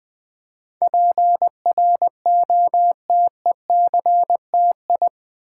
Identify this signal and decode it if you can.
Morse code, heard as a keyed tone, about 20 wpm, 700 Hz. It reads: PROTECTI
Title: Build A Word - 4 Letters At A Time - 10-Letter Words - 20wpm